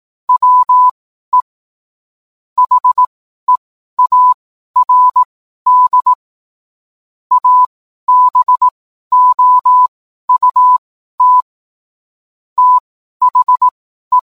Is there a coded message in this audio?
Yes: Morse code from 0.3 to 14.2 s, heard as a keyed tone, about 18 wpm, spaced as for 12 wpm, 1000 Hz. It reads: WE HEARD ABOUT THE